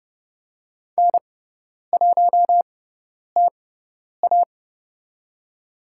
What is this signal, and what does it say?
Morse code, heard as a keyed tone, about 30 wpm, spaced as for 10 wpm, 690 Hz. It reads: N1TA
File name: Lesson 18g - Letter m - Call Signs 30wpm